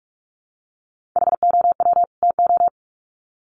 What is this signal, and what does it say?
Morse code, heard as a keyed tone, about 44 wpm, 700 Hz. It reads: HOW TO
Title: Sets of 2 Words 45wpm